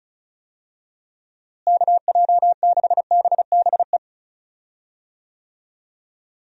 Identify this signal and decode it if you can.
Morse code, heard as a keyed tone, about 36 wpm, 700 Hz. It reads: KJ6BBE
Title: USA Call Signs - 6 Characters - Repeated 3x - 35wpm